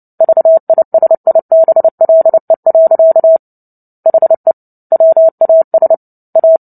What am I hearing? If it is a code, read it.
Morse code, heard as a keyed tone, about 30 wpm, 650 Hz. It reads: VISIBLE. HE WAS A
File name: Book Two / 03-Chapter 3 - 29 WPM